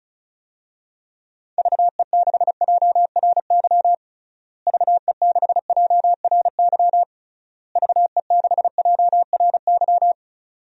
Morse code, heard as a keyed tone, about 36 wpm, 700 Hz. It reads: VE6JRY VE6JRY VE6JRY
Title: DXCC-100 Call Signs - 6 Characters - Repeated 3x - 35wpm